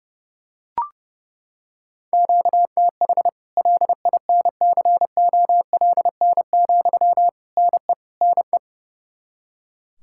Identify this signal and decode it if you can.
Morse code, heard as a keyed tone, about 30 wpm, 700 Hz. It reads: QTH LINCOLN, NE NE
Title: Practice QSO - Exchange 1 - Beginning - 30wpm